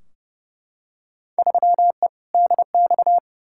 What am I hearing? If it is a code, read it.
Morse code, heard as a keyed tone, about 32 wpm, 700 Hz. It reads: 3E DX
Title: Field Day Exchanges Encore - 30wpm